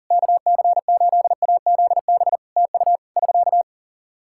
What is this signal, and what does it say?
Morse code, heard as a keyed tone, about 40 wpm, 700 Hz. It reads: KC8AZB TU <SK>
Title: Practice QSO - Exchange 4 - Ending - 40wpm